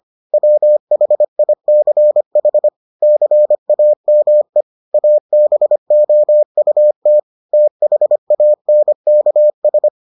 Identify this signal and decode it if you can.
Morse code, heard as a keyed tone, about 25 wpm, 600 Hz. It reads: WHICH CAME ABOUT THANKS